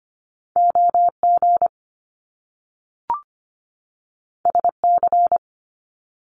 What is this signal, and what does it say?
Morse code, heard as a keyed tone, about 25 wpm, 700 Hz. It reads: OG SC
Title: Top Two Letter Combinations in Words 25wpm